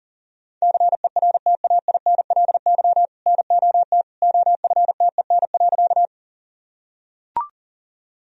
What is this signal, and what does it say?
Morse code, heard as a keyed tone, about 40 wpm, 700 Hz. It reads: CERTAINLY NOT OFTEN.
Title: Sentences from Top 1000 Words - Review Part 1 - 40wpm